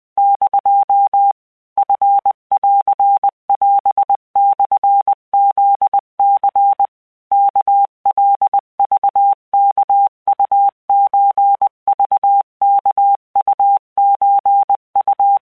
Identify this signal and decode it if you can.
Morse code (Japanese Wabun code), heard as a keyed tone, about 20 wpm, 800 Hz. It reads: <DO> チンオモフニ ワカクワウソクワウソウ
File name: keuikuchiyokugo20wptan